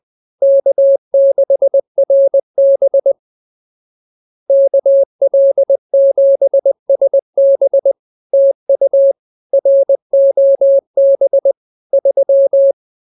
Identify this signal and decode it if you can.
Morse code, heard as a keyed tone, about 20 wpm, 550 Hz. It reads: K6RB KL7SB TU ROB 3
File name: CWT-201-20